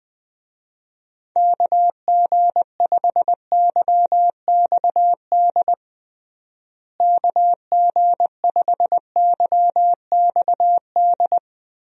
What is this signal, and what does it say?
Morse code, heard as a keyed tone, about 20 wpm, 700 Hz. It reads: KG5YXD KG5YXD
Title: USA Call Signs - 6 Characters - Repeated 3x - 20wpm